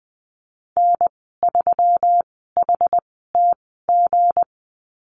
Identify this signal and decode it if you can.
Morse code, heard as a keyed tone, about 20 wpm, spaced as for 15 wpm, 700 Hz. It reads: N3HTG
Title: 167 Lesson 27g - Number 7 - Call Signs 20wpm